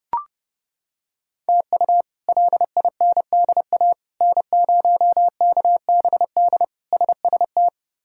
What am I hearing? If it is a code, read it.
Morse code, heard as a keyed tone, about 30 wpm, 700 Hz. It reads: TU LINDA N0KBD SST
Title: K1USN SST QSO Practice - 30wpm